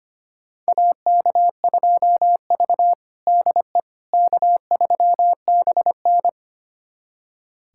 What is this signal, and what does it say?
Morse code, heard as a keyed tone, about 25 wpm, 700 Hz. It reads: AK2V DE K3BN
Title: USA Callsign DE Callsign - 4 Characters - 25wpm